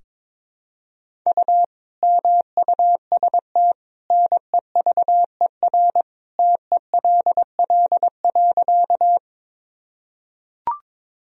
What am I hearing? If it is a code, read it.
Morse code, heard as a keyed tone, about 22 wpm, 700 Hz. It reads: U MUST NEVER TELL.
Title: Sentences from Top 300 Words and Abbreviations 22wpm